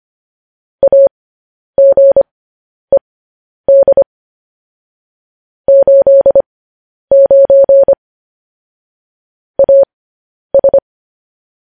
Morse code, heard as a keyed tone, about 25 wpm, spaced as for 10 wpm, 550 Hz. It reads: AGED 89 AS
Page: news headlines in morse code at 10 wpm